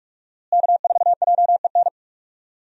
Morse code, heard as a keyed tone, about 46 wpm, 700 Hz. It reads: K4JEN
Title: USA Call Signs - 5 Characters - Repeated 3x - 45wpm